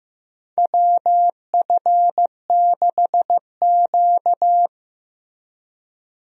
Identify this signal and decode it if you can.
Morse code, heard as a keyed tone, about 15 wpm, 700 Hz. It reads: WF6Q
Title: USA Call Signs - 4 Characters - Repeated 3x - 15wpm